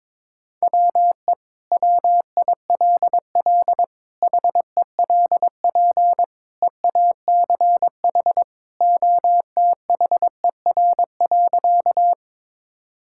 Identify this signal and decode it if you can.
Morse code, heard as a keyed tone, about 22 wpm, 700 Hz. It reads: WE WILL HELP EACH OTHER.
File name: practice / 22wpm / Sentences with 2-Word Phrases - 22wpm